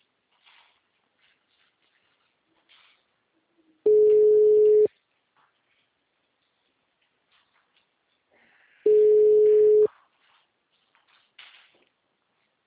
Ringback tone (Europe, 425 Hz).